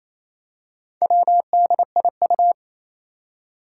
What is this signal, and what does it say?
Morse code, heard as a keyed tone, about 29 wpm, 700 Hz. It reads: WDIU
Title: Four Challenging Letter-Number Combinations 28wpm